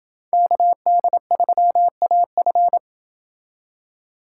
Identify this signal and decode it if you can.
Morse code, heard as a keyed tone, about 29 wpm, 700 Hz. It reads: KD3AF